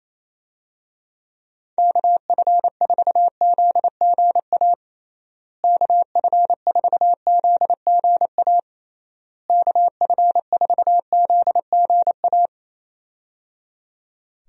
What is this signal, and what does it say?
Morse code, heard as a keyed tone, about 28 wpm, 700 Hz. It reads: KF4ZGA KF4ZGA KF4ZGA
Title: USA Call Signs - 6 Characters - Repeated 3x - 28wpm